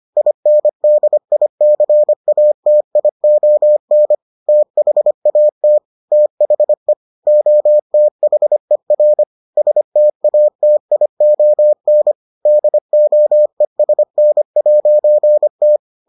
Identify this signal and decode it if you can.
Morse code, heard as a keyed tone, about 25 wpm, 600 Hz. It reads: INDICATION THAT THE OTHER STATION DOESN'T